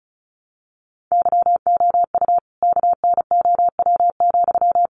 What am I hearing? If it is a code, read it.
Morse code, heard as a keyed tone, about 34 wpm, 700 Hz. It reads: YOU KNOW,
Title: Sentences from Top 200 Words 35wpm